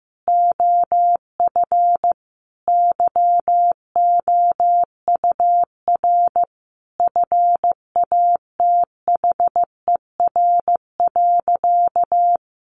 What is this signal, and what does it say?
Morse code, heard as a keyed tone, about 15 wpm, 700 Hz. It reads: OF YOUR FATHER.
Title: Sentences from Top 1300 Words 15wpm